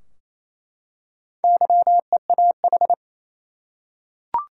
Morse code, heard as a keyed tone, about 28 wpm, 700 Hz. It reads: YEAH
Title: Four Letter Words - 28wpm